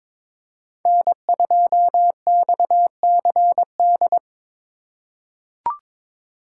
Morse code, heard as a keyed tone, about 22 wpm, 690 Hz. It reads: N2XCD
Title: USA Call Signs - 5 Characters - 2x Character Spacing - 22wpm